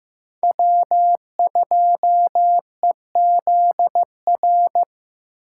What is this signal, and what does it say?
Morse code, heard as a keyed tone, about 15 wpm, 700 Hz. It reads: W2EZR
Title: USA Call Signs - 5 Characters - 6x Character Spacing - 15wpm